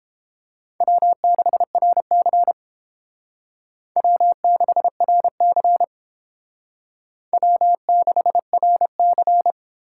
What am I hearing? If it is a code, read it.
Morse code, heard as a keyed tone, about 30 wpm, 700 Hz. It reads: W6RC W6RC W6RC